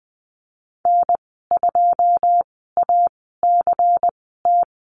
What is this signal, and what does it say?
Morse code, heard as a keyed tone, about 20 wpm, spaced as for 14 wpm, 700 Hz. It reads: N2ACT
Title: 140 Lesson 23g - Letter f - Call Signs 20wpm